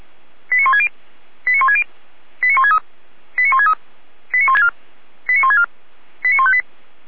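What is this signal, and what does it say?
Five-tone selective call (ZVEI), heard as quick runs of five tones, tones of about 70 ms. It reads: 89160 89160 89163 89163 89164 89164 89158
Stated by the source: ZVEI five-tone call, code 89130, received by SDR.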